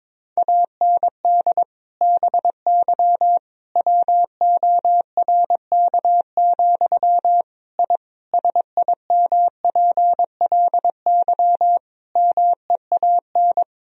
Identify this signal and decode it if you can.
Morse code, heard as a keyed tone, about 22 wpm, 700 Hz. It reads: AND BY WORK, I SIMPLY MEAN
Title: Sentences from Top 1000 Words - Review Part 4 - 22wpm